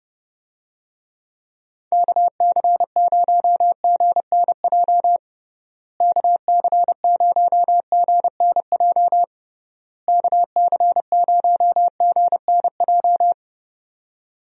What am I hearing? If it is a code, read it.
Morse code, heard as a keyed tone, about 32 wpm, 700 Hz. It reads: KC0GNJ KC0GNJ KC0GNJ
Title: USA Call Signs - 6 Characters - Repeated 3x - 30wpm